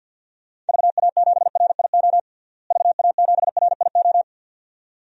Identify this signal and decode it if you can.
Morse code, heard as a keyed tone, about 52 wpm, 700 Hz. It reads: VA7RIO VA7RIO
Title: DXCC-100 Call Signs - 6 Characters - Repeated 2x - 50wpm